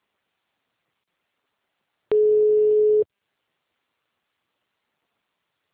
Ringback tone (Europe, 425 Hz).
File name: out- -2021- - - .2158